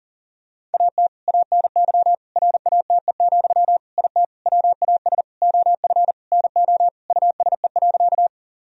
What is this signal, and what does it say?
Morse code, heard as a keyed tone, about 40 wpm, 700 Hz. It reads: AT ANY RATE, IT WAS OF NO USE.
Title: Sentences from Top 1000 Words - Review Part 4 - 40wpm